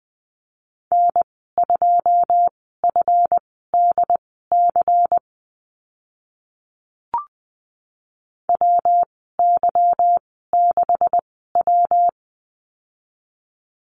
Morse code, heard as a keyed tone, about 20 wpm, spaced as for 14 wpm, 700 Hz. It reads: N2FDC WY6W
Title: 161 Lesson 26g - Letter g - Call Signs 20wpm